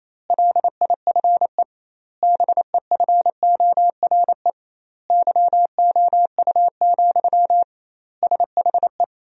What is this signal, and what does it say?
Morse code, heard as a keyed tone, about 28 wpm, 700 Hz. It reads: LIFE BEFORE YOU, SHE
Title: 2x Word-Spacing - Sentences from Top 400 Words - 28wpm